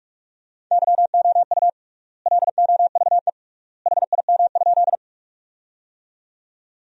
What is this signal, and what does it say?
Morse code, heard as a keyed tone, about 46 wpm, 700 Hz. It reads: YOU LOVE HIM?
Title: 3x Word-Spacing - Sentences from Top 400 Words - 45wpm